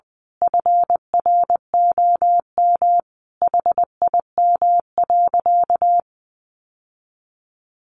Morse code, heard as a keyed tone, about 20 wpm, 700 Hz. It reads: FROM HIM.
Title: Sentences from Top 1500 Words 20wpm